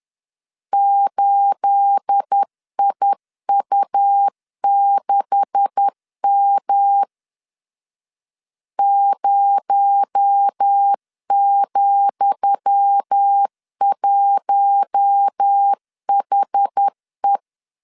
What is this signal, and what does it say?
Morse code, heard as a keyed tone, about 10 wpm, 800 Hz. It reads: 8IU6M 0,1HE
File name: LEZIONI / 10 SIGNS / 10D COMA 10wpm 3013